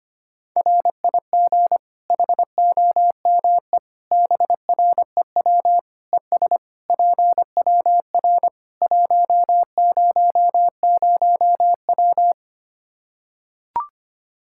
Morse code, heard as a keyed tone, about 25 wpm, 700 Hz. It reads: RIG HOME BREW ES PWR 100W
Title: Practice QSO - Exchange 2 - Info - 25wpm